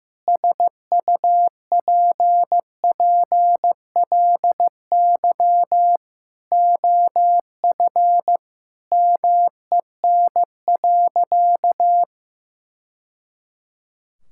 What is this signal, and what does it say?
Morse code, heard as a keyed tone, about 15 wpm, 700 Hz. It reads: SUPPLY OF MEN.
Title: Sentences from Top 1000 Words - Review Part 5 - 15wpm